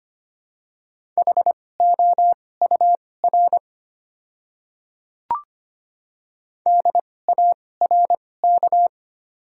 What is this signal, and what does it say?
Morse code, heard as a keyed tone, about 25 wpm, spaced as for 18 wpm, 700 Hz. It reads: HOUR DARK
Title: 2x Character Spacing - Top 500 Words - 25wpm